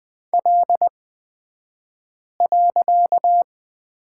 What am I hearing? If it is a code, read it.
Morse code, heard as a keyed tone, about 20 wpm, spaced as for 5 wpm, 700 Hz. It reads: L.
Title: 254 Lesson 41d - Period - Two Characters 20wpm